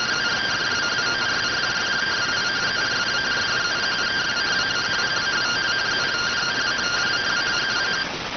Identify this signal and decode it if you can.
Two-tone teleprinter signal, about 170 Hz shift.